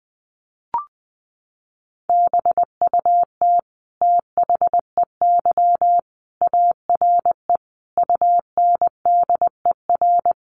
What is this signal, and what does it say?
Morse code, heard as a keyed tone, about 20 wpm, 700 Hz. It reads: BUT THEY ARE UNDER
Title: Sentences from Top 1300 Words 20wpm